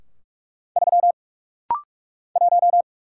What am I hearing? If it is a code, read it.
Morse code, heard as a keyed tone, about 44 wpm, 700 Hz. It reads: 3 1
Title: Single Letter-Number - Rapid-Fire 45wpm